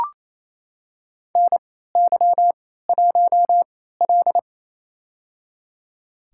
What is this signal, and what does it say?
Morse code, heard as a keyed tone, about 28 wpm, spaced as for 16 wpm, 700 Hz. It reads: NY1L
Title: USA Call Signs - 4 Characters - 3x Character Spacing - 28wpm